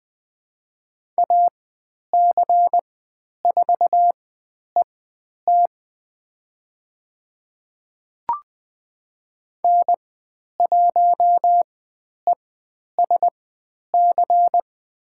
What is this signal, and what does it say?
Morse code, heard as a keyed tone, about 20 wpm, spaced as for 10 wpm, 700 Hz. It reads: AC4ET N1ESC
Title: 095 Lesson 16g - Letter c - Call Signs 20wpm